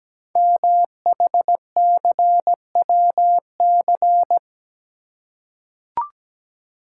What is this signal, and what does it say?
Morse code, heard as a keyed tone, about 17 wpm, 700 Hz. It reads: MHCWC
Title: Code Groups - 5 Letter:Numbers - 17wpm